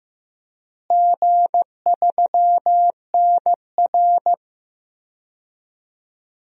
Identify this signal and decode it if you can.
Morse code, heard as a keyed tone, about 15 wpm, 700 Hz. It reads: G3NR
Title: DXCC-100 Call Signs - 4 Characters - Repeated 3x - 15wpm